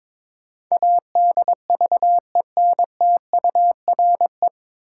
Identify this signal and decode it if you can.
Morse code, heard as a keyed tone, about 22 wpm, 700 Hz. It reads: ADVENTURE
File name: 9 Letter Words - First 3 Letters Given - 22wpm